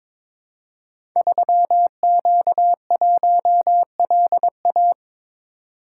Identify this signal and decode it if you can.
Morse code, heard as a keyed tone, about 23 wpm, 700 Hz. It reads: 3Q1LA
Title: Code Groups - 5 Letter:Numbers - 22wpm